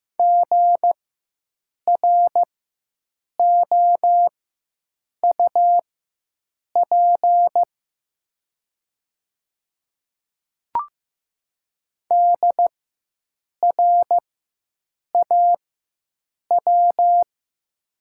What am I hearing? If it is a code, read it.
Morse code, heard as a keyed tone, about 15 wpm, spaced as for 7 wpm, 700 Hz. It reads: GROUP DRAW
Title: 4x Character Spacing - Top 300 Words - 15wpm